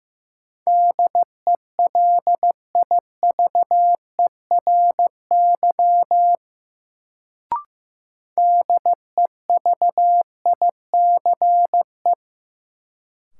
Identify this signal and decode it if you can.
Morse code, heard as a keyed tone, about 15 wpm, 700 Hz. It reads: DELIVERY DEVICE
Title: Sentences from Top 1500 Words - New Word Drill Alt - 15wpm